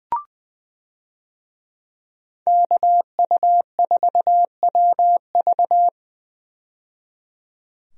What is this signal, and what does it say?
Morse code, heard as a keyed tone, about 21 wpm, 700 Hz. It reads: KU4WV
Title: USA Call Signs - 5 Characters - 20wpm